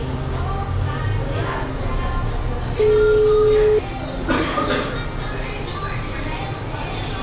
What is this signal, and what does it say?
Ringback tone (Europe, 425 Hz), with speech.